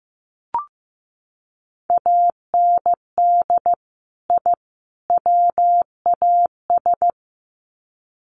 Morse code, heard as a keyed tone, about 15 wpm, 700 Hz. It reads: AND I WAS
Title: Sets of 3 Words 15wpm